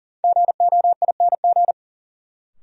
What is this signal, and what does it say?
Morse code, heard as a keyed tone, about 40 wpm, 700 Hz. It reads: GOING
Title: Sentences from Top 500 Words - New Word Drill - 40wpm